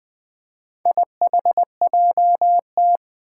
Morse code, heard as a keyed tone, about 20 wpm, 700 Hz. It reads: IHJT
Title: Four Challenging Letter-Number Combinations 20wpm